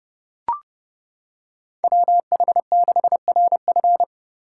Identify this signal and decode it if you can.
Morse code, heard as a keyed tone, about 30 wpm, 700 Hz. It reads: WH6RF